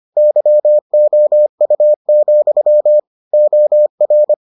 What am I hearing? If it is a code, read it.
Morse code, heard as a keyed tone, about 25 wpm, 600 Hz. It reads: YOU, OR